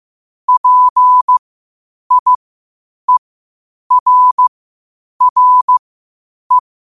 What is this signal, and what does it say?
Morse code, heard as a keyed tone, about 15 wpm, spaced as for 8 wpm, 1000 Hz. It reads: PIERRE
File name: qotdfiles / 8wpm